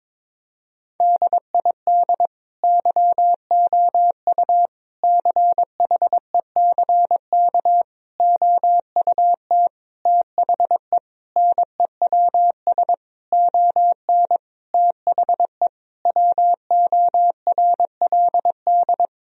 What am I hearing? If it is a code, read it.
Morse code, heard as a keyed tone, about 22 wpm, 700 Hz. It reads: DID YOU CHECK OUT THE NEWS ON THE WORLD
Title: Sentences from Top 1000 Words - Review Part 4 - 22wpm